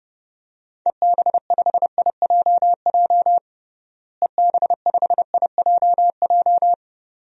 Morse code, heard as a keyed tone, about 30 wpm, 700 Hz. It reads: EB5IJJ EB5IJJ
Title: DXCC-100 Call Signs - 6 Characters - Repeated 2x - 30wpm